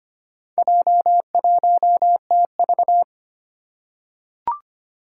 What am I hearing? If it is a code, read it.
Morse code, heard as a keyed tone, about 25 wpm, 700 Hz. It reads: J1TV
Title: Four Challenging Letter-Number Combinations 25wpm